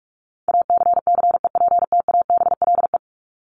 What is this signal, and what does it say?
Morse code, heard as a keyed tone, about 44 wpm, 700 Hz. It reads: ACCEPTABLE
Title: Top 1501-1750 Words 45wpm